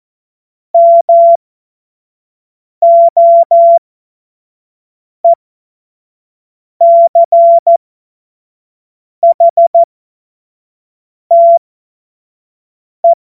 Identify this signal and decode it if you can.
Morse code, heard as a keyed tone, about 14 wpm, spaced as for 5 wpm, 680 Hz. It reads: MOECHTE
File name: Morsekurs / Witze / Witz 16 Januar 2024 25 F70